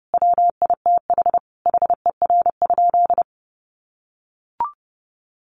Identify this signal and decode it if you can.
Morse code, heard as a keyed tone, about 30 wpm, 700 Hz. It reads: WITH HER?
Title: Sentences from Top 600 Words 30wpm